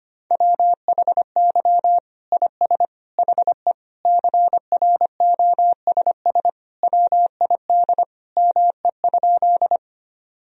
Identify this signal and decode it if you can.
Morse code, heard as a keyed tone, about 25 wpm, 700 Hz. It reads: WHY IS HE CROSS WID ME?
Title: Sentences from Top 300 Words and Abbreviations 25wpm